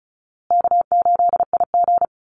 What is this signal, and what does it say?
Morse code, heard as a keyed tone, about 34 wpm, 700 Hz. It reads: K8IG
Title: USA Call Signs Repeated 3x 35wpm